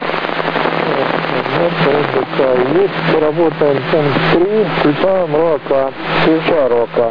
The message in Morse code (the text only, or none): none